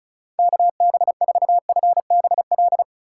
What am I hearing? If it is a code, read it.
Morse code, heard as a keyed tone, about 36 wpm, 700 Hz. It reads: KB4FBL
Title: USA Call Signs - 6 Characters - 4x Character Spacing - 35wpm